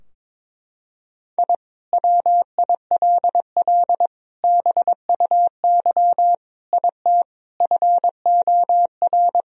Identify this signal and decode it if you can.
Morse code, heard as a keyed tone, about 22 wpm, 700 Hz. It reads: I WILL BUY IT FOR
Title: Sentences from Top 1000 Words - Review Part 4 - 22wpm